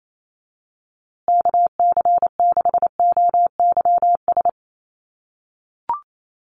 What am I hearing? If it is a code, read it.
Morse code, heard as a keyed tone, about 28 wpm, 700 Hz. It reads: KC6OYS
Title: USA Call Signs Repeated 2x 28wpm